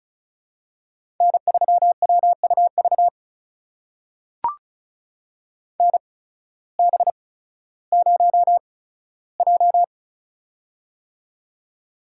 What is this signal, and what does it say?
Morse code, heard as a keyed tone, about 36 wpm, spaced as for 16 wpm, 700 Hz. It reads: N3WUV NB0J
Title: USA Call Signs - 8x Character Spacing - 35wpm